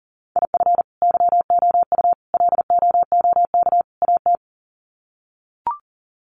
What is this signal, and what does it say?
Morse code, heard as a keyed tone, about 37 wpm, 700 Hz. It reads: IF YOU LOOK AT
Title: Sets of 4 Words 40wpm